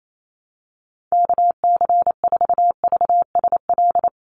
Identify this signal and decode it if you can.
Morse code, heard as a keyed tone, about 28 wpm, 700 Hz. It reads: KC4VSL
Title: USA Call Signs 28wpm